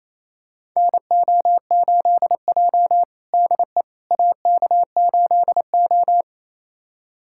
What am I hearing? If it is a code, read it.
Morse code, heard as a keyed tone, about 28 wpm, 700 Hz. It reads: NO8J DE AK8O
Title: USA Callsign DE Callsign - 4 Characters - 28wpm